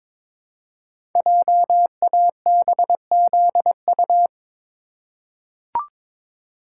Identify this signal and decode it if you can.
Morse code, heard as a keyed tone, about 22 wpm, 700 Hz. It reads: JABZU
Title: Code Groups - 5 Letters - 22wpm